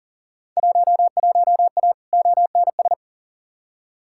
Morse code, heard as a keyed tone, about 40 wpm, 700 Hz. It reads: 11A ONS
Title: Field Day Exchanges Encore - 40wpm